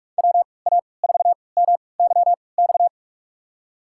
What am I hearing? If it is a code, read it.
Morse code, heard as a keyed tone, about 47 wpm, spaced as for 26 wpm, 700 Hz. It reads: WA4MYX
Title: USA Call Signs - 6 Characters - 3x Character Spacing - 45wpm